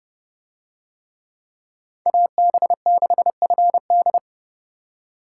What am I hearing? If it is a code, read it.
Morse code, heard as a keyed tone, about 29 wpm, 700 Hz. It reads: AB6FD